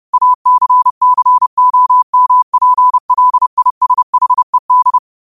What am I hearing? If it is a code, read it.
Morse code, heard as a keyed tone, about 30 wpm, 1000 Hz. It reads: ACCOMPLISHED
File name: qotdfiles / 30wpm